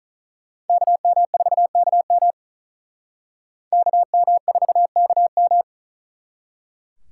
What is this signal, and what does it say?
Morse code, heard as a keyed tone, about 45 wpm, 700 Hz. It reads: KM4KM KM4KM